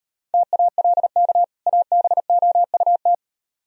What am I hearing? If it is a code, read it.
Morse code, heard as a keyed tone, about 38 wpm, 700 Hz. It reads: TALK ABOUT